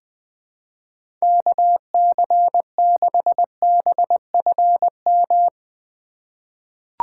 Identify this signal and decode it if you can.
Morse code, heard as a keyed tone, about 20 wpm, 700 Hz. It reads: KC6BFM
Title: USA Call Signs - 6 Characters - 4x Character Spacing - 20wpm